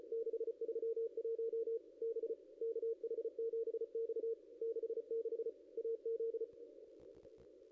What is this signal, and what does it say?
Morse code, heard as a keyed tone, about 34 wpm, 470 Hz. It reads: 631 B KH7X 66 AZ
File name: contests / ss-cw-k5tr-op-n5rz -0600